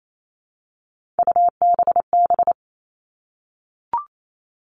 Morse code, heard as a keyed tone, about 28 wpm, 700 Hz. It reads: UBB
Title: All Cut Numbers - 3-Digit 28wpm